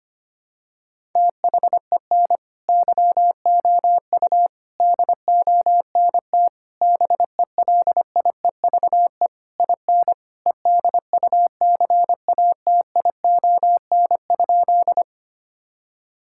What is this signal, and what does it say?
Morse code, heard as a keyed tone, about 25 wpm, 700 Hz. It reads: THEN YOU DONT BELIEVE IN EDUCATION?
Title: Sentences from Top 1000 Words - Review Part 6 - 25wpm